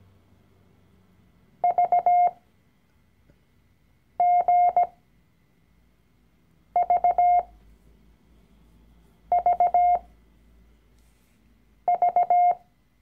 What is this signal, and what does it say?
Morse code, heard as a keyed tone, about 17 wpm, spaced as for 4 wpm, 710 Hz. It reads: VGVVV
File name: CW-Online-Kurs / Anfänger / Lektion 012